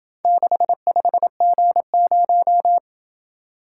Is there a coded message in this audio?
Yes: Morse code from 0.2 to 2.8 s, heard as a keyed tone, about 27 wpm, 700 Hz. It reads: 65G0